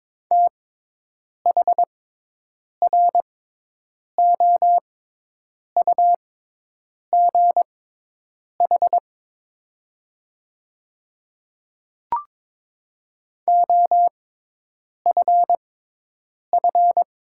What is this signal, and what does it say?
Morse code, heard as a keyed tone, about 22 wpm, spaced as for 8 wpm, 700 Hz. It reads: THROUGH OFF
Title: 6x Character Spacing - Top 200 Words - 22wpm